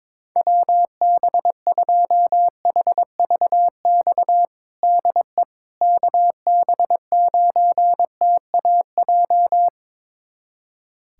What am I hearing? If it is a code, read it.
Morse code, heard as a keyed tone, about 22 wpm, 700 Hz. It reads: WB2HVX DE KB9TAJ
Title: USA Callsign DE Callsign - 6 Characters - 22wpm